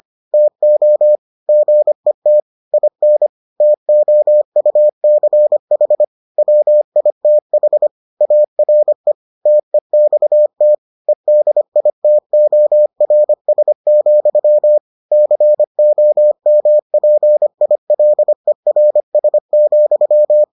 Morse code, heard as a keyed tone, about 25 wpm, 600 Hz. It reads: TO GET IN TOUCH WITH ARE TEXT EDITORS, COMPILERS,